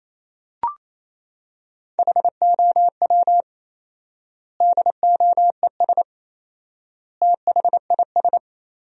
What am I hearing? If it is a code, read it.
Morse code, heard as a keyed tone, about 28 wpm, 700 Hz. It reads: HOW DOES THIS
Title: 4x Word-Spacing - Sentences from Top 400 Words - 28wpm